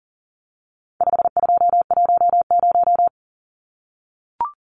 Morse code, heard as a keyed tone, about 40 wpm, 700 Hz. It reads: 5210
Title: Numbers - 4-Digit 40wpm